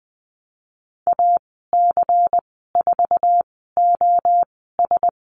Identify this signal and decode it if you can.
Morse code, heard as a keyed tone, about 20 wpm, spaced as for 14 wpm, 700 Hz. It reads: AC4OS
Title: 128 Lesson 21g - Number 6 - Call Signs 20wpm